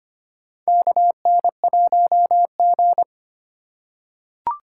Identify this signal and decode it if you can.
Morse code, heard as a keyed tone, about 25 wpm, 700 Hz. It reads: KN1G
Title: USA Call Signs - 4 Characters - 4x Character Spacing - 25wpm